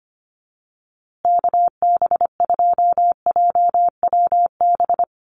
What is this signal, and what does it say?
Morse code, heard as a keyed tone, about 25 wpm, 700 Hz. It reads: KB2JWB
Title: USA Call Signs Repeated 3x 25wpm